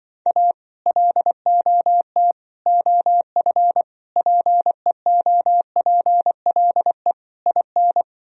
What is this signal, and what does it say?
Morse code, heard as a keyed tone, about 24 wpm, 700 Hz. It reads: A LOT OF PEOPLE IN
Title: Speed-Racing - Sets of 5 Words 20wpm